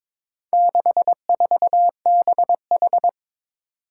Morse code, heard as a keyed tone, about 21 wpm, 700 Hz. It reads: 64BH
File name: practice / 22wpm / Four Letter-Number Combinations - 22wpm